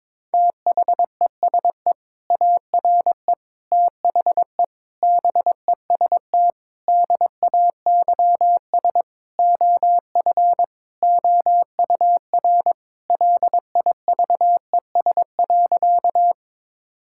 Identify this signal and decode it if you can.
Morse code, heard as a keyed tone, about 22 wpm, 700 Hz. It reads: THESE ARE THE BEST DAYS OF OUR LIVES.
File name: practice / 22wpm / Sentences with 2-Word Phrases - 22wpm